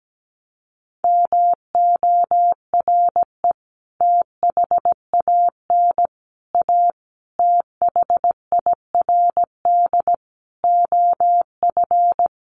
Morse code, heard as a keyed tone, about 17 wpm, 700 Hz. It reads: MORE THAN A THIRD OF
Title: Sets of 5 Words Encore2 17wpm